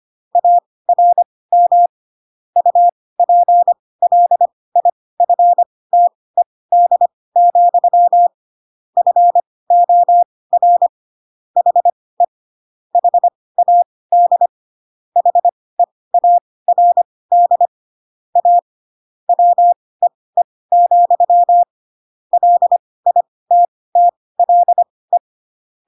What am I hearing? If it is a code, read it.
Morse code, heard as a keyed tone, about 25 wpm, spaced as for 18 wpm, 700 Hz. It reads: ARM UPLIFTED, FOR HE HAD HEARD A WEE, LITTLE
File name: Pinocchio / Pinocchio ch1 12 25 ik1hge